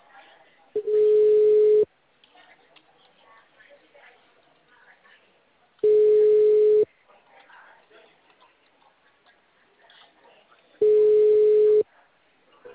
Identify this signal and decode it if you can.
Ringback tone (Europe, 425 Hz), with speech.